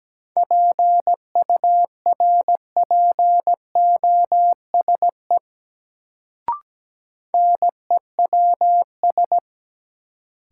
Morse code, heard as a keyed tone, about 17 wpm, 700 Hz. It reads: PURPOSE NEWS
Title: Sets of 5 Words - New Word Drill - 17wpm